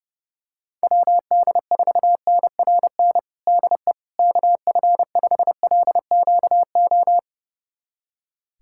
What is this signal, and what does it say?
Morse code, heard as a keyed tone, about 30 wpm, 700 Hz. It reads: WD4NRN DE KF5LQO
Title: USA Callsign DE Callsign - 6 Characters - 30wpm